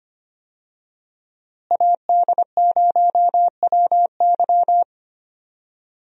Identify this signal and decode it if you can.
Morse code, heard as a keyed tone, about 25 wpm, 700 Hz. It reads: AD0WY
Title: USA Call Signs - 5 Characters - Repeated 3x - 25wpm